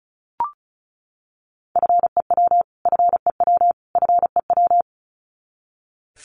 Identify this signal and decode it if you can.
Morse code, heard as a keyed tone, about 34 wpm, 700 Hz. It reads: FEW FEW FEW
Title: Top 300 Words Repeated 35wpm